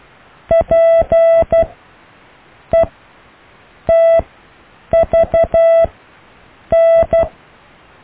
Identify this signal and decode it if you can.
Morse code, heard as a keyed tone, about 12 wpm, spaced as for 7 wpm, 660 Hz. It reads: PETVN
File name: cw / EKTPV+BFHNW